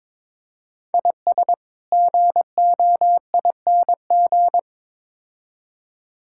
Morse code, heard as a keyed tone, about 22 wpm, 700 Hz. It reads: IS GOING
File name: practice / 22wpm / Build A Phrase - 5-Word Phrases - 22wpm